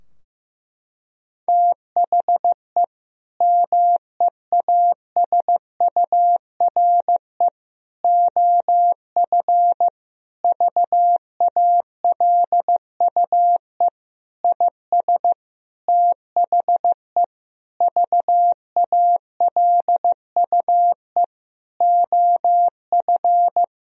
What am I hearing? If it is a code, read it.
Morse code, heard as a keyed tone, about 15 wpm, 700 Hz. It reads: THE MEASURE OF VALUE IS THE VALUE OF
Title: Sentences from Top 1000 Words - Review Part 3 - 15wpm